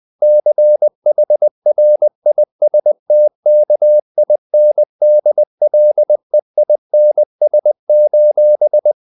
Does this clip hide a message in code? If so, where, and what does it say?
Yes: Morse code from 0.2 to 8.9 s, heard as a keyed tone, about 20 wpm, 600 Hz. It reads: CHRISTKINDLEINS: